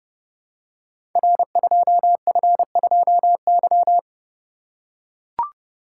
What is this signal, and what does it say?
Morse code, heard as a keyed tone, about 30 wpm, 700 Hz. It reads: R2F2Y
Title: Code Groups - 5 Letter:Numbers - 30wpm